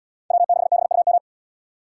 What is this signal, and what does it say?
Morse code, heard as a keyed tone, about 70 wpm, 700 Hz. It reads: W6BUM